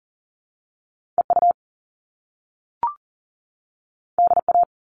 Morse code, heard as a keyed tone, about 41 wpm, 700 Hz. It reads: EU DA
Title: All Cut Numbers - 2-Digit 40wpm